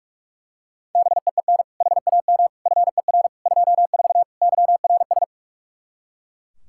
Morse code, heard as a keyed tone, about 41 wpm, 700 Hz. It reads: BEEN HAM FER 24 YRS
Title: Practice QSO - Exchange 3 - More Info - 45wpm